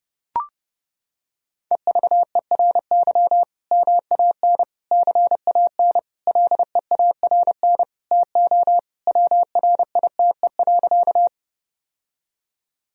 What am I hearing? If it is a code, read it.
Morse code, heard as a keyed tone, about 32 wpm, 700 Hz. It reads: EVERY MAN CAN LEARN TO WRITE.
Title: Sentences from Top 1000 Words - Review Part 1 - 30wpm